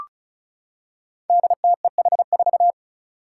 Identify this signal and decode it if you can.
Morse code, heard as a keyed tone, about 36 wpm, 700 Hz. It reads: DTEH4
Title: Five Challenging Letter-Number Combinations 35wpm